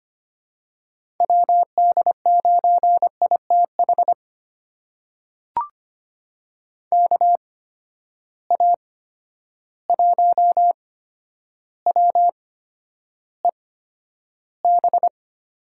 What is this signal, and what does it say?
Morse code, heard as a keyed tone, about 25 wpm, spaced as for 11 wpm, 700 Hz. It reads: WD9ITH KA1WEB
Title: USA Call Signs - 6 Characters - 8x Character Spacing - 25wpm